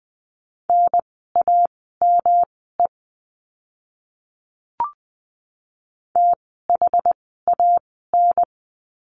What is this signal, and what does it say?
Morse code, heard as a keyed tone, about 20 wpm, spaced as for 14 wpm, 700 Hz. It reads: NAME THAN
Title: 108 Lesson 18f - Letter m - Words 20wpm